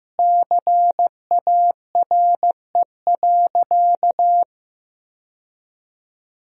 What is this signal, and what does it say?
Morse code, heard as a keyed tone, about 15 wpm, 700 Hz. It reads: CARE.
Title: 4x Word-Spacing - Sentences from Top 300 Words - 15wpm